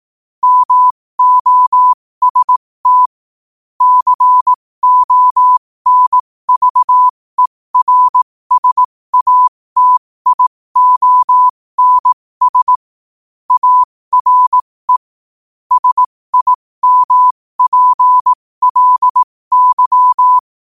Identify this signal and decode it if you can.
Morse code, heard as a keyed tone, about 18 wpm, spaced as for 15 wpm, 1000 Hz. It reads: MOST CONVERSATIONS ARE SIMPLY
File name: qotdfiles / 15wpm